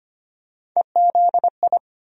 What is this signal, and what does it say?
Morse code, heard as a keyed tone, about 25 wpm, 700 Hz. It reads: EZI
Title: USA Call Sign Postfixes - 3 Letters 25wpm